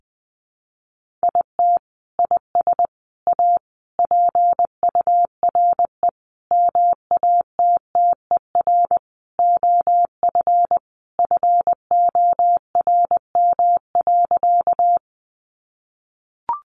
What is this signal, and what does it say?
Morse code, heard as a keyed tone, about 20 wpm, 700 Hz. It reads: IT IS A PURE MATTER OF FORM.
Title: Sentences from Top 1500 Words 20wpm